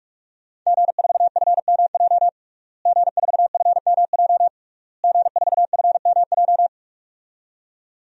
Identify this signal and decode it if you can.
Morse code, heard as a keyed tone, about 41 wpm, 700 Hz. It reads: G4FMJ G4FMJ G4FMJ
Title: DXCC-100 Call Signs - 5 Characters - Repeated 3x - 45wpm